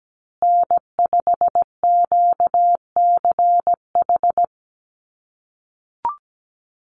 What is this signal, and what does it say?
Morse code, heard as a keyed tone, about 17 wpm, 700 Hz. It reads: N5QCH
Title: USA Call Signs Repeated 3x 17wpm